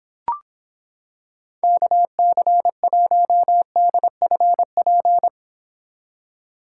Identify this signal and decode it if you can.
Morse code, heard as a keyed tone, about 26 wpm, 700 Hz. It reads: KC1DFP